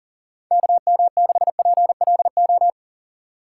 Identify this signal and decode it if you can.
Morse code, heard as a keyed tone, about 40 wpm, 700 Hz. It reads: KM6PLO
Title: USA Call Signs - 6 Characters - 3x Character Spacing - 40wpm